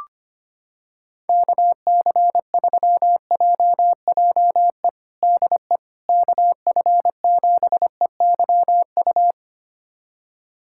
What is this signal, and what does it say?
Morse code, heard as a keyed tone, about 25 wpm, 700 Hz. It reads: KC3JJE DE KF7EYU
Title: USA Callsign DE Callsign 25wpm